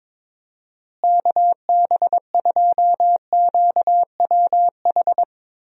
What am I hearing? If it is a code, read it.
Morse code, heard as a keyed tone, about 22 wpm, 700 Hz. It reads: KB2QWH
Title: USA Call Signs - 6 Characters - Repeated 3x - 22wpm